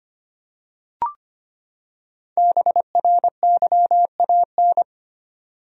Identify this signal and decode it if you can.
Morse code, heard as a keyed tone, about 25 wpm, 700 Hz. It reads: BRYAN
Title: USA Common Names - 25wpm